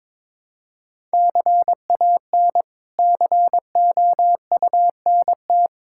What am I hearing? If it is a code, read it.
Morse code, heard as a keyed tone, about 22 wpm, 700 Hz. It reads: CAN COUNT
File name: practice / 22wpm / Sets of 8 Words - 22wpm